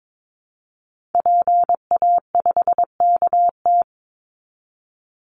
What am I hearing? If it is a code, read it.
Morse code, heard as a keyed tone, about 22 wpm, 700 Hz. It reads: PA5KT
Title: Call Signs Repeated 3x - 22wpm